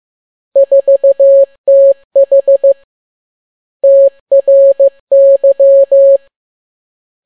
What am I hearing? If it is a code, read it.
Morse code, heard as a keyed tone, about 15 wpm, 550 Hz. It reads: 4TH TRY
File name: n6tt p1 15C13